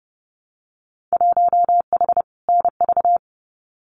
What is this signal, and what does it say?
Morse code, heard as a keyed tone, about 30 wpm, 700 Hz. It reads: 1H NV
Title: Winter Field Day Exchanges 30wpm